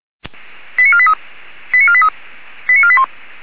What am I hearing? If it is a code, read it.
Five-tone selective call (ZVEI), heard as quick runs of five tones, tones of about 70 ms. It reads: 89483 89483 89481